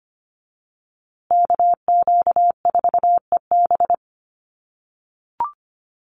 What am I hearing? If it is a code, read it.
Morse code, heard as a keyed tone, about 25 wpm, 700 Hz. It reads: KQ4EB
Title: USA Call Signs Repeated 3x 25wpm